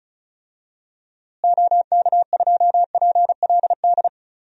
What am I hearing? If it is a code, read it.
Morse code, heard as a keyed tone, about 36 wpm, 700 Hz. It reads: OK2PLD
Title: DXCC-100 Call Signs - 6 Characters - Repeated 3x - 35wpm